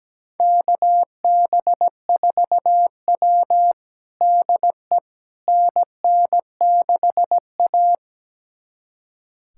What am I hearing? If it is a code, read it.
Morse code, heard as a keyed tone, about 17 wpm, 700 Hz. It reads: KB4W DE NN6A
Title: USA Callsign DE Callsign - 4 Characters - 17wpm